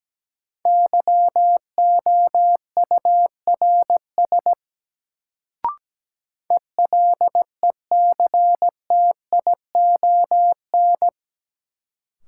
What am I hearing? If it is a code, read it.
Morse code, heard as a keyed tone, about 17 wpm, 700 Hz. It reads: YOURS ELECTION
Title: Sentences from Top 1500 Words - New Word Drill Alt - 17wpm